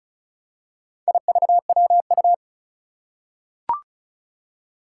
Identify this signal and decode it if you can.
Morse code, heard as a keyed tone, about 36 wpm, 700 Hz. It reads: IVWU
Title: Four Challenging Letter-Number Combinations 35wpm